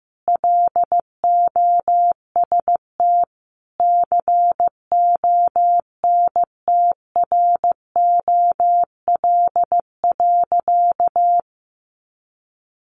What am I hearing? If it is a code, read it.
Morse code, heard as a keyed tone, about 15 wpm, 700 Hz. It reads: LOST CONTROL.
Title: Sentences from Top 1400 Words 15wpm